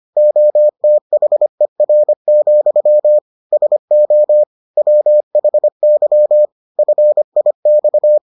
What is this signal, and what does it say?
Morse code, heard as a keyed tone, about 25 wpm, 600 Hz. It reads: OTHER, SO WHY FIX